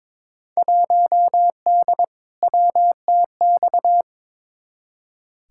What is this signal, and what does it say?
Morse code, heard as a keyed tone, about 22 wpm, 700 Hz. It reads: 1D WTX
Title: Field Day Exchanges - 22wpm